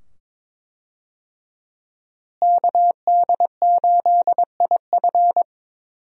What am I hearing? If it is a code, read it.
Morse code, heard as a keyed tone, about 22 wpm, 700 Hz. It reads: KD8IF
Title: USA Call Signs - 5 Characters - Repeated 2x - 22wpm